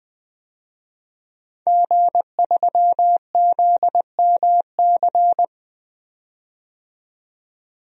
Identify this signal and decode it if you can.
Morse code, heard as a keyed tone, about 20 wpm, 700 Hz. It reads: G3ZMC
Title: DXCC-100 Call Signs - 5 Characters - 20wpm